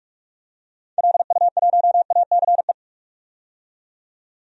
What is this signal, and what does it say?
Morse code, heard as a keyed tone, about 45 wpm, 700 Hz. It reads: LU1ACE
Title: DXCC-100 Call Signs - 6 Characters - 45wpm